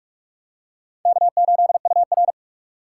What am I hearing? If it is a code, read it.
Morse code, heard as a keyed tone, about 46 wpm, 700 Hz. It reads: K8UR
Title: USA Call Signs - 4 Characters - 6x Character Spacing - 45wpm